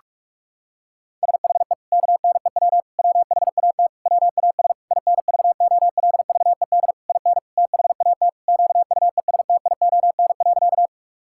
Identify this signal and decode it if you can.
Morse code, heard as a keyed tone, about 41 wpm, 700 Hz. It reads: SHE KNEW WHAT WAS INVOLVED IN THAT QUESTION.
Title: Sentences from Top 1000 Words - Review Part 5 - 45wpm